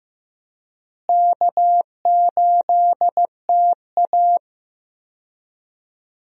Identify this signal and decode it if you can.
Morse code, heard as a keyed tone, about 15 wpm, 700 Hz. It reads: K8TA
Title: USA Call Signs - 4 Characters - Repeated 2x - 15wpm